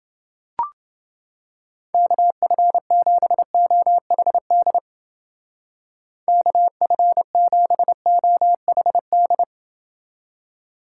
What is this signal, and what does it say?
Morse code, heard as a keyed tone, about 28 wpm, 700 Hz. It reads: KF7OHD KF7OHD